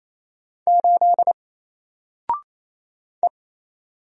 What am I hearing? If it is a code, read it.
Morse code, heard as a keyed tone, about 28 wpm, 700 Hz. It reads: 8 E
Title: Single Letter-Number Alt 28wpm